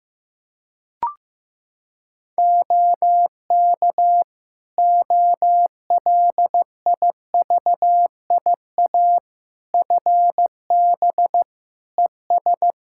Morse code, heard as a keyed tone, about 15 wpm, 700 Hz. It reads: OK OLIVIA FB ES
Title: Practice QSO - Exchange 2 - Info - 15wpm